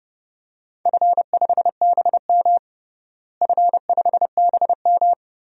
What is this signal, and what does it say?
Morse code, heard as a keyed tone, about 30 wpm, 700 Hz. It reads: F5BM F5BM
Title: DXCC-100 Call Signs - 4 Characters - Repeated 3x - 30wpm